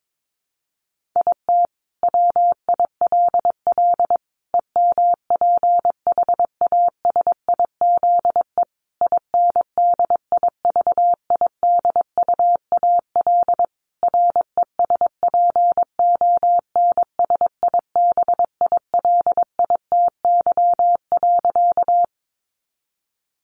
Morse code, heard as a keyed tone, about 22 wpm, 700 Hz. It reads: IT WILL EMPHASIZE INDIVIDUAL RESPONSIBILITY.
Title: Sentences from Top 1500 Words 22wpm